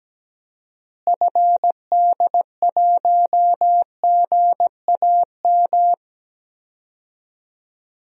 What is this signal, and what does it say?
Morse code, heard as a keyed tone, about 17 wpm, 700 Hz. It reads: FD1GAM
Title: DXCC-100 Call Signs - 6 Characters - Repeated 2x - 17wpm